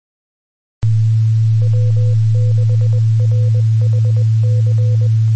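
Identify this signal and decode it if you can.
Morse code, heard as a keyed tone, about 20 wpm, 500 Hz, with noise. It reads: W6RHC
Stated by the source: Received on the 2 m band at 146.850 MHz.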